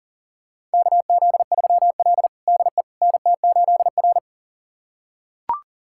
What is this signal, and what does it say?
Morse code, heard as a keyed tone, about 40 wpm, 700 Hz. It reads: KZ3L DE NT8R
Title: USA Callsign DE Callsign - 4 Characters - 40wpm